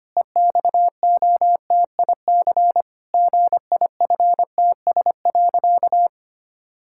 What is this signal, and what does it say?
Morse code, heard as a keyed tone, about 25 wpm, 700 Hz. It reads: EXOTIC GIFTS.
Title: Pangram Challenge v2 - 25wpm